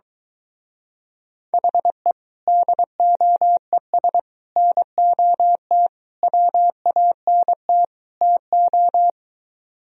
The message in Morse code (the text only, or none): HE DOES NOT WANT TO